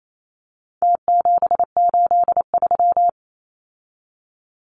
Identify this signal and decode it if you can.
Morse code, heard as a keyed tone, about 28 wpm, 700 Hz. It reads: T783
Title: Numbers - Contest Style - 4-Digit 28wpm